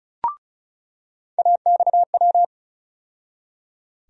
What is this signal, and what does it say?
Morse code, heard as a keyed tone, about 36 wpm, 700 Hz. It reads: AXW
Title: USA Call Sign Postfixes - 3 Letters 35wpm